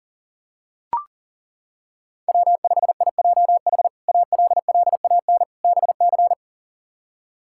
Morse code, heard as a keyed tone, about 37 wpm, 690 Hz. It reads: W5IJH ALLAN BC
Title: K1USN SST QSO Practice - 40wpm